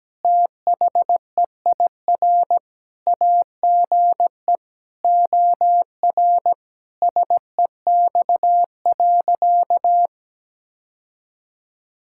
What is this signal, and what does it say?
Morse code, heard as a keyed tone, about 17 wpm, 700 Hz. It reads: THEIR AGE OR SEX.
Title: Sentences from Top 1600 Words 17wpm